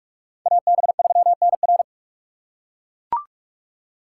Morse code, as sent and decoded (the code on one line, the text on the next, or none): .- -... ...-- -. .-.
AB3NR